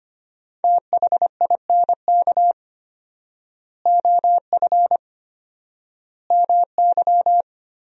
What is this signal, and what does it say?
Morse code, heard as a keyed tone, about 25 wpm, 700 Hz. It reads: THINK OF MY
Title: 4x Word-Spacing - Sentences from Top 300 Words - 25wpm